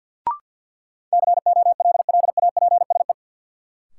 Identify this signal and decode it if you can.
Morse code, heard as a keyed tone, about 49 wpm, 700 Hz. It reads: COLLAPSE
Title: Sets of 5 Words Encore - New Word Drill - 50wpm